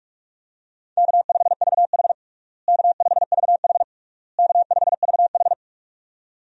Morse code, heard as a keyed tone, about 41 wpm, 700 Hz. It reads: K5VH K5VH K5VH
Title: USA Call Signs - 4 Characters - Repeated 3x - 45wpm